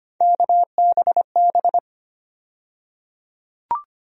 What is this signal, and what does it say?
Morse code, heard as a keyed tone, about 25 wpm, 700 Hz. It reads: KBB
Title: USA Call Sign Postfixes - 3 Letters 25wpm